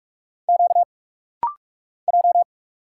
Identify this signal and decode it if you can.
Morse code, heard as a keyed tone, about 44 wpm, 700 Hz. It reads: Q J
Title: Single Letter-Number - Rapid-Fire 45wpm